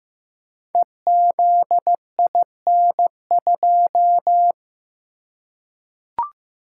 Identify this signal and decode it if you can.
Morse code, heard as a keyed tone, about 15 wpm, 700 Hz. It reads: EZIN2
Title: Code Groups - 5 Letter:Numbers - 15wpm